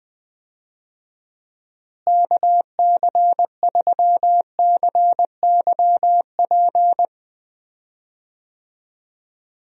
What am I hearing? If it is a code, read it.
Morse code, heard as a keyed tone, about 21 wpm, 700 Hz. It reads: KC3CYP
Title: USA Call Signs - 6 Characters - Repeated 3x - 20wpm